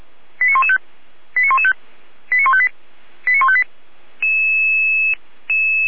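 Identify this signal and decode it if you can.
Five-tone selective call (ZVEI), heard as quick runs of five tones, tones of about 70 ms. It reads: 89115 89115 89158 89158
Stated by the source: ZVEI five-tone selective call, code 89080, received with an SDR.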